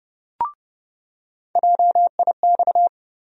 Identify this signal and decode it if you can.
Morse code, heard as a keyed tone, about 30 wpm, 700 Hz. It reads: JIX
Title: USA Call Sign Postfixes - 3 Letters 30wpm